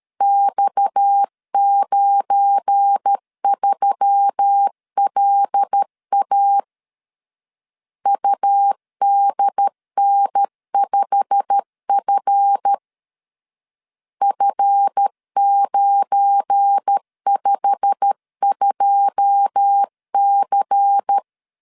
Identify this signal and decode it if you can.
Morse code, heard as a keyed tone, about 13 wpm, 800 Hz. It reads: X93LA UDN5F F952C
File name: LEZIONI / 9 2 12wpm 2301